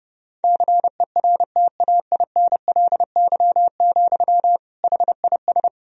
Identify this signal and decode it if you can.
Morse code, heard as a keyed tone, about 32 wpm, 700 Hz. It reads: CERTAINLY, HIS
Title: Sentences from Top 1000 Words - Review Part 2 - 30wpm